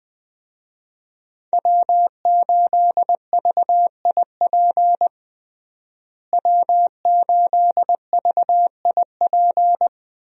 Morse code, heard as a keyed tone, about 20 wpm, 700 Hz. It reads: W8VIP W8VIP
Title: USA Call Signs - 5 Characters - Repeated 2x - 20wpm